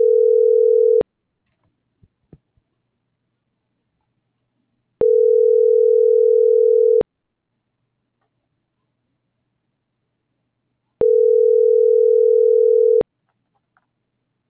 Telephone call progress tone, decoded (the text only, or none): ringback tone